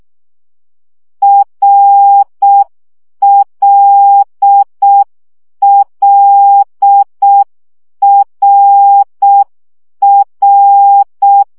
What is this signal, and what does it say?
Morse code, heard as a keyed tone, about 6 wpm, 800 Hz. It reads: RLLRR
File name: CW gyakorlas r l a2 5 baud